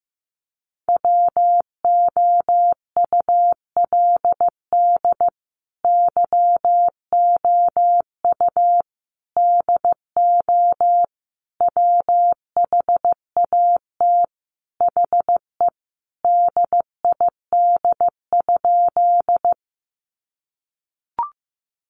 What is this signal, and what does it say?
Morse code, heard as a keyed tone, about 15 wpm, 700 Hz. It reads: WOULD YOU DO WHAT HE DID?
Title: Sentences from Top 100 Words 15wpm